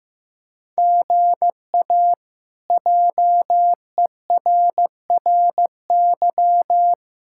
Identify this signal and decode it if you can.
Morse code, heard as a keyed tone, about 15 wpm, 700 Hz. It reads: GA JERRY
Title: K1USN SST QSO Practice - 15wpm